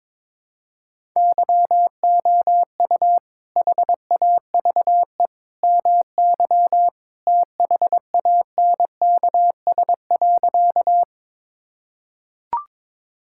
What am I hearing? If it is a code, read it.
Morse code, heard as a keyed tone, about 22 wpm, 700 Hz. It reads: YOU HAVE MY THANKS.
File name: practice / 22wpm / Sentences with 2-Word Phrases - 22wpm